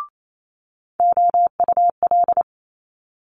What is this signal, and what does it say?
Morse code, heard as a keyed tone, about 28 wpm, 700 Hz. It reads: OUL
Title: Multi-letter Phonemes Quick Fire 28wpm